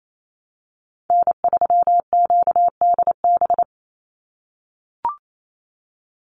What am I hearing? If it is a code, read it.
Morse code, heard as a keyed tone, about 28 wpm, 700 Hz. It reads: N3QDB
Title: USA Call Signs 28wpm